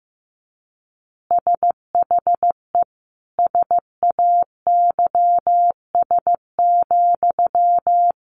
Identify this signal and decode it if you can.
Morse code, heard as a keyed tone, about 14 wpm, 700 Hz. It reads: SHE SAYS,
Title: Easy Phrases 15wpm